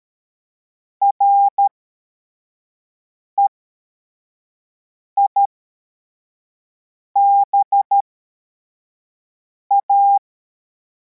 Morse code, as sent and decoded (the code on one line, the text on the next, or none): .-. .. -... .-
REIBA